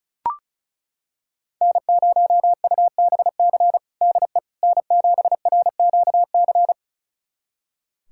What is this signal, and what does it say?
Morse code, heard as a keyed tone, about 34 wpm, 700 Hz. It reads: N0UBC DE N7RQC
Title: USA Callsign DE Callsign - 5 Characters - 35wpm